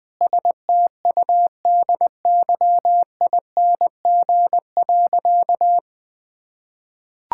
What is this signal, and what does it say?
Morse code, heard as a keyed tone, about 21 wpm, 700 Hz. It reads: STUDYING.
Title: Sentences with 6-Word Phrases - 20wpm